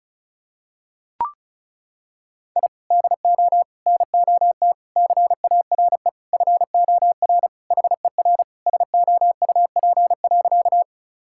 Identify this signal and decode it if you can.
Morse code, heard as a keyed tone, about 36 wpm, 700 Hz. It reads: I DO NOT CARE FOR HER SOUP.
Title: Sentences from Top 1000 Words - Review Part 6 - 35wpm